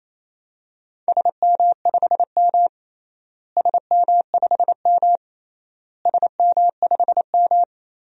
Morse code, heard as a keyed tone, about 28 wpm, 700 Hz. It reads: SM5M SM5M SM5M
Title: DXCC-100 Call Signs - 4 Characters - Repeated 3x - 28wpm